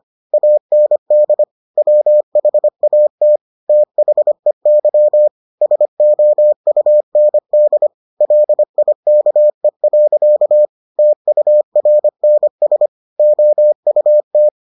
Morse code, heard as a keyed tone, about 25 wpm, 600 Hz. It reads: AND WHAT THEY SOUND LIKE. TURNS OUT